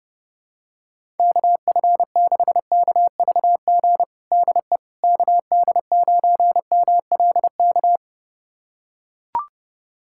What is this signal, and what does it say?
Morse code, heard as a keyed tone, about 30 wpm, 700 Hz. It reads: KF6KVG DE KD9MLK
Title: USA Callsign DE Callsign 30wpm